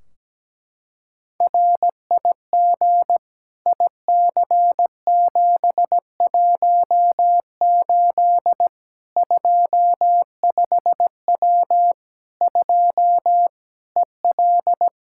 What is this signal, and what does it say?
Morse code, heard as a keyed tone, about 17 wpm, 700 Hz. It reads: RIG IC718 25W 2 EL
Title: Practice QSO - Exchange 2 - Info - 17wpm